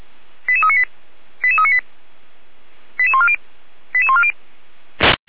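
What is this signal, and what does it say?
Five-tone selective call (ZVEI), heard as quick runs of five tones, tones of about 70 ms. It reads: 88398 88398 88140 88140
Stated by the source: ZVEI five-tone selective call received by SDR, from Feuerwehr Orlamünde.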